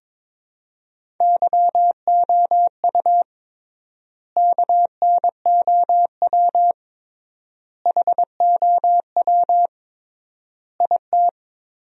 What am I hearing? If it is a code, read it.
Morse code, heard as a keyed tone, about 23 wpm, 700 Hz. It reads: YOU KNOW HOW IT
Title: 3x Word-Spacing - Sentences from Top 100 Words - 22wpm